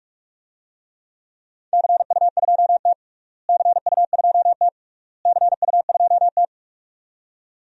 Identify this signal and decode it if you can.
Morse code, heard as a keyed tone, about 47 wpm, 700 Hz. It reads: CU2T CU2T CU2T
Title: DXCC-100 Call Signs - 4 Characters - Repeated 3x - 45wpm